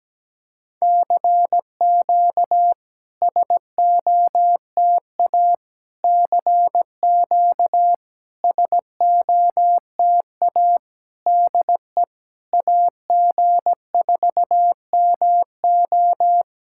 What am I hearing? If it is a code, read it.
Morse code, heard as a keyed tone, about 17 wpm, 700 Hz. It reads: CQ SOTA CQ SOTA DE AG4MO
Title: SOTA QSO Practice 17wpm